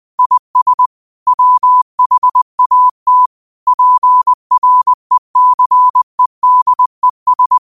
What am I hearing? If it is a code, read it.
Morse code, heard as a keyed tone, about 20 wpm, 1000 Hz. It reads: IS WHAT PRECEDES
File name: qotdfiles / 20wpm